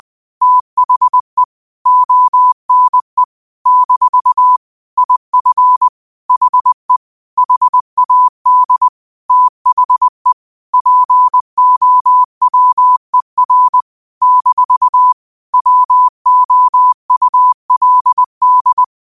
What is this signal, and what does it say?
Morse code, heard as a keyed tone, about 20 wpm, 1000 Hz. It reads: THE ONE - IF HE HAD THE POWER - WOULD
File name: qotdfiles / 20wpm